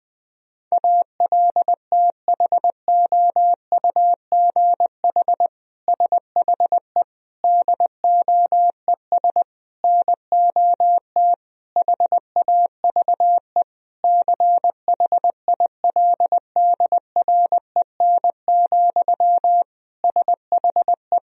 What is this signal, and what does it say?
Morse code, heard as a keyed tone, about 19 wpm, 700 Hz. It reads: ALTHOUGH SHE DOES NOT HAVE CHILDREN, SHE
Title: Sentences from Top 1000 Words - Review Part 3 - 20wpm